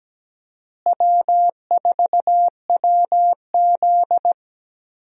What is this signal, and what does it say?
Morse code, heard as a keyed tone, about 17 wpm, 700 Hz. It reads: W4WZ
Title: USA Call Signs - 4 Characters - Repeated 3x - 17wpm